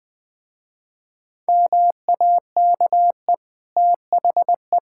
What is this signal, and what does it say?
Morse code, heard as a keyed tone, about 19 wpm, 700 Hz. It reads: MAKE THE
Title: Easy Sentences with Hints - 20wpm